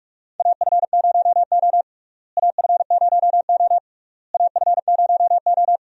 Morse code, heard as a keyed tone, about 47 wpm, 700 Hz. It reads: AF0O AF0O AF0O
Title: USA Call Signs - 4 Characters - Repeated 3x - 45wpm